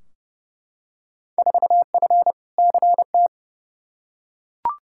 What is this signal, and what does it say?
Morse code, heard as a keyed tone, about 32 wpm, 700 Hz. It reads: 4F CT
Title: Field Day Exchanges Encore - 30wpm